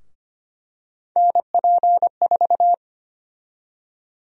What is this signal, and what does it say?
Morse code, heard as a keyed tone, about 25 wpm, 700 Hz. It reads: NP4
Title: USA Call Sign Prefixes - Repeated 2x - 25wpm